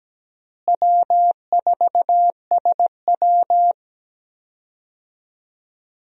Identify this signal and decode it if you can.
Morse code, heard as a keyed tone, about 17 wpm, 700 Hz. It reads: W4SW
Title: USA Call Signs - 4 Characters - Repeated 3x - 17wpm